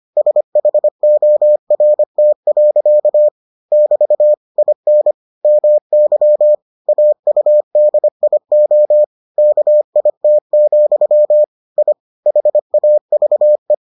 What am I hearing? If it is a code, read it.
Morse code, heard as a keyed tone, about 25 wpm, 600 Hz. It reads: SHORT. <BT> IN MY AUDIO KIT, I HAVE